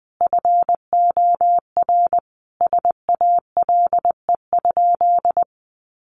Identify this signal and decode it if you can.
Morse code, heard as a keyed tone, about 20 wpm, 700 Hz. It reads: FOR SALE?
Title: Sentences from Top 1400 Words 20wpm